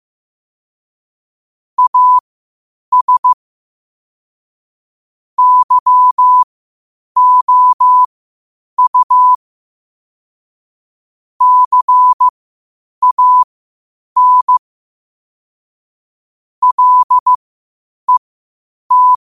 Morse code, heard as a keyed tone, about 15 wpm, spaced as for 8 wpm, 1000 Hz. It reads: AS YOU CAN LET
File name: 8wpm